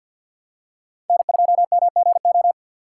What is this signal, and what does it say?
Morse code, heard as a keyed tone, about 52 wpm, 700 Hz. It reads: N2MGO
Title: USA Call Signs - 5 Characters - 6x Character Spacing - 50wpm